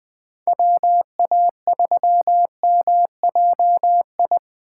Morse code, heard as a keyed tone, about 24 wpm, 700 Hz. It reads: WA3MJI
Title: Speed Racing - USA 2x3 Call Signs 20wpm